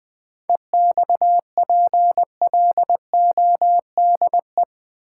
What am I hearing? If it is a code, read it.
Morse code, heard as a keyed tone, about 20 wpm, 700 Hz. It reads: EXPLODE
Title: 7 Letter Words - First 3 Letters Given - 20wpm